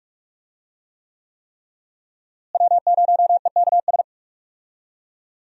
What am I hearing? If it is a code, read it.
Morse code, heard as a keyed tone, about 45 wpm, 700 Hz. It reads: W0EKS
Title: USA Call Signs - 5 Characters - Repeated 3x - 45wpm